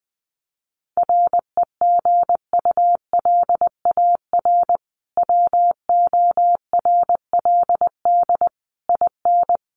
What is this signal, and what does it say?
Morse code, heard as a keyed tone, about 20 wpm, 700 Hz. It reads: REGULAR WORLD IN
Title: Sentences from Top 900 Words 20wpm